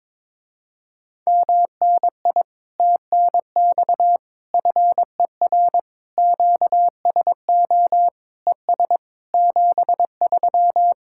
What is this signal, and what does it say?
Morse code, heard as a keyed tone, about 22 wpm, 700 Hz. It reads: MNI TNX FER QSO ES 73
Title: Speed-Racing Common QSO Phrases - 22wpm